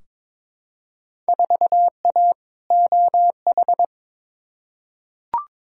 Morse code, heard as a keyed tone, about 22 wpm, 700 Hz. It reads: 4A OH
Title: Field Day Exchanges - 22wpm